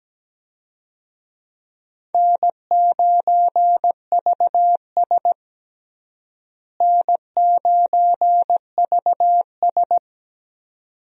Morse code, heard as a keyed tone, about 17 wpm, 700 Hz. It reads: N9VS N9VS
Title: USA Call Signs - 4 Characters - Repeated 2x - 17wpm